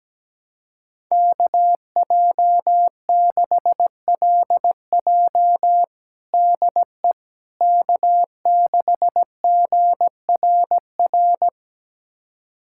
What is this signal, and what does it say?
Morse code, heard as a keyed tone, about 17 wpm, 700 Hz. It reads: KJ6LJ DE K6GRR
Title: USA Callsign DE Callsign - 5 Characters - 17wpm